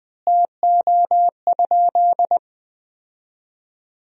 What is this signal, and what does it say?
Morse code, heard as a keyed tone, about 20 wpm, 700 Hz. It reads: TO?
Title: 2x Word-Spacing - Sentences from Top 100 Words - 20wpm